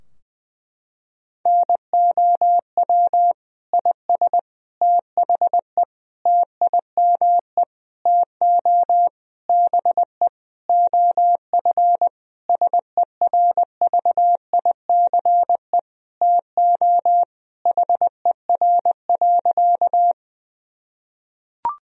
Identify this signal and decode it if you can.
Morse code, heard as a keyed tone, about 20 wpm, 700 Hz. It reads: NOW IS THE TIME TO BE OF SERVICE TO HER.
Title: Sentences from Top 1000 Words - Review Part 4 - 20wpm